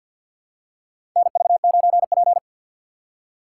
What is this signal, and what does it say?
Morse code, heard as a keyed tone, about 47 wpm, 700 Hz. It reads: NV9P
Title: USA Call Signs - 3x Character Spacing - 50wpm